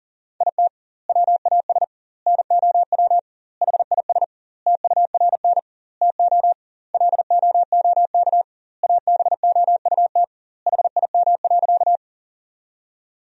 Morse code, heard as a keyed tone, about 40 wpm, 700 Hz. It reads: IT WAS NOW HIS TURN TO LOOK ABOUT HIM.
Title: 2x Word-Spacing - Sentences from Top 200 Words - 40wpm